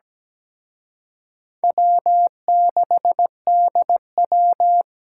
Morse code, heard as a keyed tone, about 20 wpm, 700 Hz. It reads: W6DW
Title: Speed Racing - USA 1x2 Call Signs 17wpm